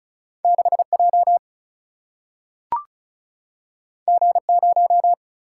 Morse code, heard as a keyed tone, about 36 wpm, 700 Hz. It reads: 6 J G 0
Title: Single-Letter Single-Number 35wpm